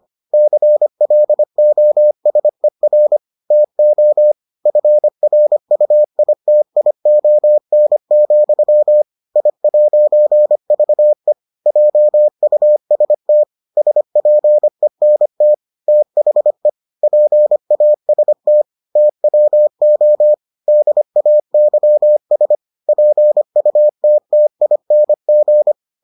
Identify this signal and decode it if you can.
Morse code, heard as a keyed tone, about 25 wpm, 600 Hz. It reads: CLOSER TO FRUITION, I'VE JUST SPENT THE PAST TWO DAYS PUTTING